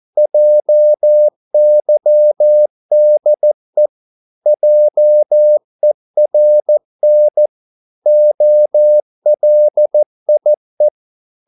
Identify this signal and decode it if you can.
Morse code, heard as a keyed tone, about 14 wpm, 600 Hz. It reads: JYDE JERN OLIE
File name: Korte ord DK 14wpm